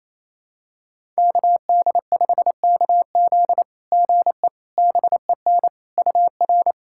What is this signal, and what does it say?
Morse code, heard as a keyed tone, about 28 wpm, 690 Hz. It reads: KD5KZ GE BEN UR
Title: SOTA QSO Practice 28wpm